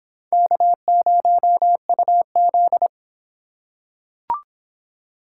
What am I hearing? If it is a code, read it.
Morse code, heard as a keyed tone, about 25 wpm, 700 Hz. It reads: K0UZ